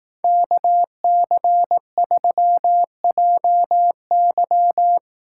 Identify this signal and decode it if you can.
Morse code, heard as a keyed tone, about 18 wpm, 700 Hz. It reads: KC3JY
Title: Speed Racing - USA 2x2 Call Signs 15wpm